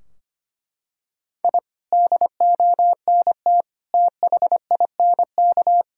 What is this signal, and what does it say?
Morse code, heard as a keyed tone, about 25 wpm, 700 Hz. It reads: I DONT THINK
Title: Sentences from Top 1000 Words - Review Part 4 - 25wpm